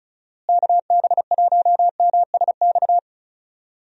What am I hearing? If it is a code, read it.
Morse code, heard as a keyed tone, about 34 wpm, 700 Hz. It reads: KB1MSX
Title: USA Call Signs - 6 Characters - Repeated 3x - 35wpm